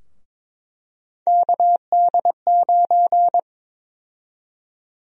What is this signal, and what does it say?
Morse code, heard as a keyed tone, about 21 wpm, 700 Hz. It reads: KD9
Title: USA Call Sign Prefixes - 22wpm